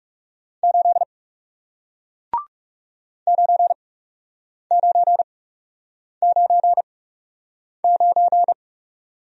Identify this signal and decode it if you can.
Morse code, heard as a keyed tone, about 39 wpm, spaced as for 8 wpm, 700 Hz. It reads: 8 9999